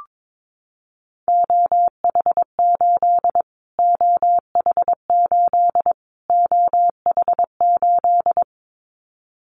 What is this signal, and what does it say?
Morse code, heard as a keyed tone, about 22 wpm, 700 Hz. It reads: OH8 OH8 OH8
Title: Callsign Prefixes for Top 25 DXCC - Repeated 3x - 22wpm